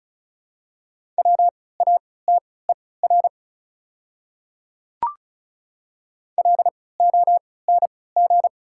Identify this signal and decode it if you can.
Morse code, heard as a keyed tone, about 36 wpm, spaced as for 20 wpm, 690 Hz. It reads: WATER LONG
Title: 3x Character Spacing - Top 100 Words - 35wpm